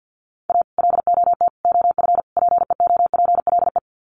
Morse code, heard as a keyed tone, about 50 wpm, 700 Hz. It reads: A LOT OF PEOPLE
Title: Sets of 4 Words 50wpm